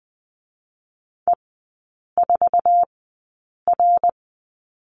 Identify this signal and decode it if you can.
Morse code, heard as a keyed tone, about 20 wpm, spaced as for 8 wpm, 690 Hz. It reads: E4R
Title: 060 Lesson 11e - Letter h - Three Characters 20wpm